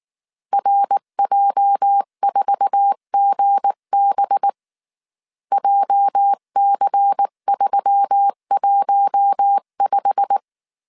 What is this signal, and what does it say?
Morse code, heard as a keyed tone, about 19 wpm, 800 Hz. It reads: RJ4GB JC315
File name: CW / DET / 1 9 1500C S B 18wpm 2551